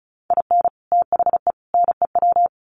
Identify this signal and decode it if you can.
Morse code, heard as a keyed tone, about 34 wpm, 700 Hz. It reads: IN THE NEW
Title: Sets of 3 Words 35wpm